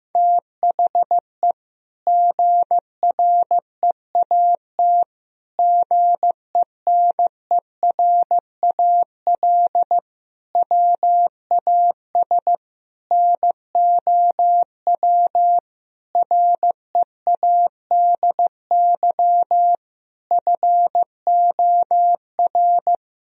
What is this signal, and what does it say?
Morse code, heard as a keyed tone, about 16 wpm, 700 Hz. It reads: THE GREAT GENERAL WAS NOW READY FOR
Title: Sentences from Top 1000 Words - Review Part 4 - 15wpm